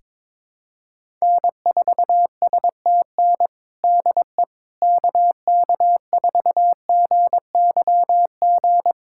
Morse code, heard as a keyed tone, about 21 wpm, 700 Hz. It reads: N4STN DE KK4GYG
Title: USA Callsign DE Callsign 22wpm